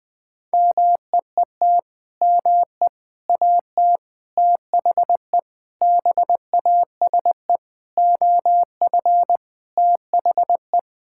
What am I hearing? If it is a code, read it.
Morse code, heard as a keyed tone, about 20 wpm, 700 Hz. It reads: MEET ME AT THE BASE OF THE
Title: Sentences with 5-Word Phrases - 20wpm